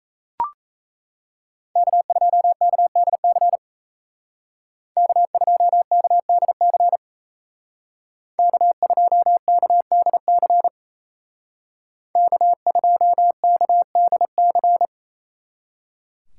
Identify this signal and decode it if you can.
Morse code, heard as a keyed tone, about 35 wpm, 700 Hz. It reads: K2KDC K2KDC K2KDC K2KDC